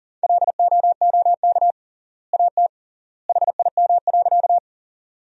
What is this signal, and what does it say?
Morse code, heard as a keyed tone, about 40 wpm, 700 Hz. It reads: LOOK AT HIM.
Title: 3x Word-Spacing - Sentences from Top 100 Words - 40wpm